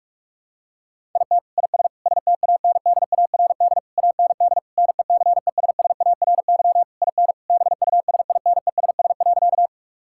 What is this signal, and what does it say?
Morse code, heard as a keyed tone, about 48 wpm, 700 Hz. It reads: IT IS STANDARD AND NECESSARY IN BUSINESS.
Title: Sentences from Top 1000 Words - Review Part 6 - 45wpm